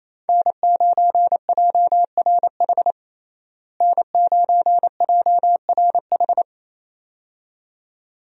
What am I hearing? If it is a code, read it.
Morse code, heard as a keyed tone, about 28 wpm, 700 Hz. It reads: N9JRH N9JRH
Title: USA Call Signs - 5 Characters - Repeated 2x - 28wpm